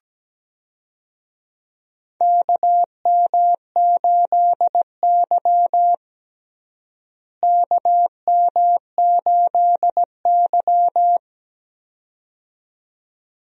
Morse code, heard as a keyed tone, about 17 wpm, 700 Hz. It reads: KM8Y KM8Y
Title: USA Call Signs - 4 Characters - Repeated 2x - 17wpm